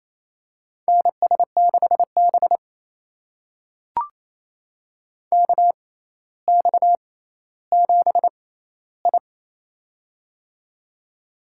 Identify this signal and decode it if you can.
Morse code, heard as a keyed tone, about 28 wpm, spaced as for 14 wpm, 700 Hz. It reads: NS6B KX7I
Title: USA Call Signs - 4 Characters - 6x Character Spacing - 28wpm